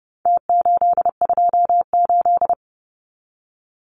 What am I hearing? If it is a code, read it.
Morse code, heard as a keyed tone, about 30 wpm, 700 Hz. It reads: T828
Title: Numbers - Contest Style - 4-Digit 30wpm